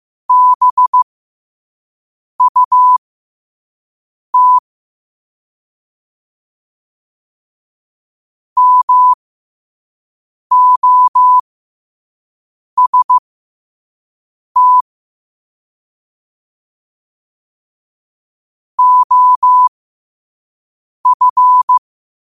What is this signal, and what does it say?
Morse code, heard as a keyed tone, about 15 wpm, spaced as for 5 wpm, 1000 Hz. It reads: BUT MOST OF
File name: qotdfiles / 5wpm